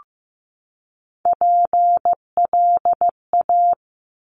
Morse code, heard as a keyed tone, about 15 wpm, 700 Hz. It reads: PLA
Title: Top Three Letter Combinations in Words 15wpm